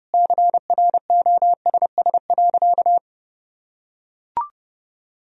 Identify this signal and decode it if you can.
Morse code, heard as a keyed tone, about 30 wpm, 700 Hz. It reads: CROSS.
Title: Sentences from Top 1000 Words - Review Part 1 - 30wpm